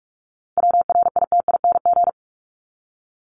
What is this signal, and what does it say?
Morse code, heard as a keyed tone, about 44 wpm, 700 Hz. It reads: WRITING
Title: Top 751-1000 Words 45wpm